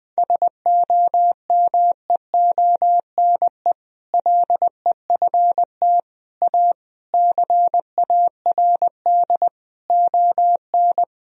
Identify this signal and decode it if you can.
Morse code, heard as a keyed tone, about 20 wpm, 700 Hz. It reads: SOMEONE LEFT A CARD ON
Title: Sentences from Top 1000 Words - Review Part 4 - 20wpm